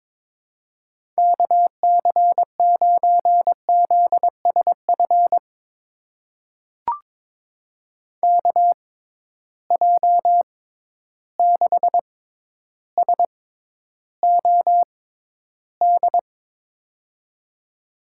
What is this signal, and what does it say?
Morse code, heard as a keyed tone, about 22 wpm, spaced as for 11 wpm, 700 Hz. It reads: KC9ZSF KJ6SOD
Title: USA Call Signs - 6 Characters - 6x Character Spacing - 22wpm